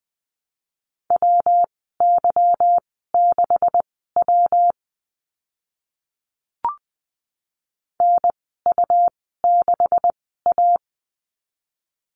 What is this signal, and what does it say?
Morse code, heard as a keyed tone, about 20 wpm, spaced as for 14 wpm, 700 Hz. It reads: WY6W NU6A
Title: 154 Lesson 25g - Letter p - Call Signs 20wpm